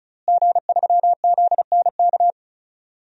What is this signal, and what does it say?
Morse code, heard as a keyed tone, about 36 wpm, 700 Hz. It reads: G3ZNK
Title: DXCC-100 Call Signs - 5 Characters - Repeated 3x - 35wpm